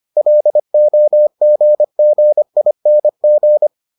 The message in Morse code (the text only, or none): LOGGING